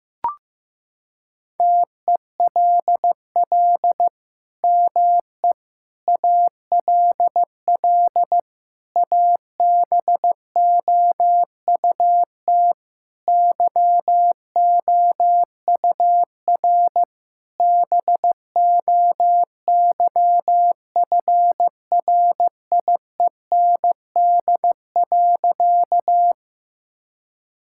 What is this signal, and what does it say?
Morse code, heard as a keyed tone, about 15 wpm, 700 Hz. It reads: TELL ME ALL ABOUT YOUR BOYFRIEND.
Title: Sentences from Top 1000 Words - Review Part 5 - 15wpm